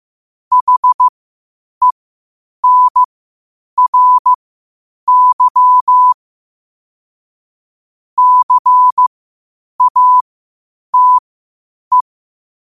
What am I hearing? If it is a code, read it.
Morse code, heard as a keyed tone, about 15 wpm, spaced as for 8 wpm, 1000 Hz. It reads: HENRY CATE
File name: qotdfiles / 8wpm